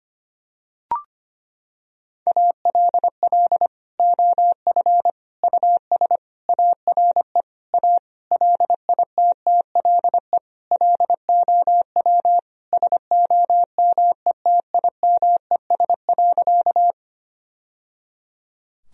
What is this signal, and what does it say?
Morse code, heard as a keyed tone, about 25 wpm, 700 Hz. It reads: ALL OF US ARE A LITTLE LOW SOMETIMES.
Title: Sentences from Top 1000 Words - Review Part 3 - 25wpm